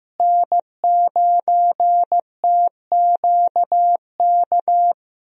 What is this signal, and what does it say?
Morse code, heard as a keyed tone, about 15 wpm, 700 Hz. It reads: N9TQK
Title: USA Call Signs - 5 Characters - Repeated 3x - 15wpm